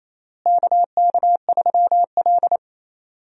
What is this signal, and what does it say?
Morse code, heard as a keyed tone, about 28 wpm, 700 Hz. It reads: KK3L
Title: USA Call Signs - 4 Characters - Repeated 2x - 28wpm